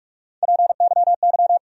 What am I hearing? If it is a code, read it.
Morse code, heard as a keyed tone, about 46 wpm, 700 Hz. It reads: PYY
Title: USA Call Sign Postfixes - 3 Letters 45wpm